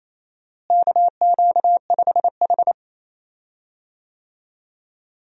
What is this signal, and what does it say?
Morse code, heard as a keyed tone, about 28 wpm, 700 Hz. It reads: KQ5H
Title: USA Call Signs - 4 Characters - Repeated 3x - 28wpm